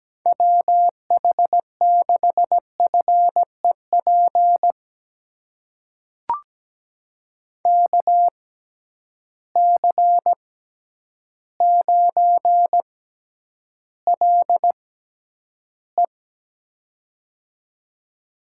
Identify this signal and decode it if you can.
Morse code, heard as a keyed tone, about 17 wpm, spaced as for 9 wpm, 700 Hz. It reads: WH6FEP KC9LE
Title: USA Call Signs - 6x Character Spacing - 17wpm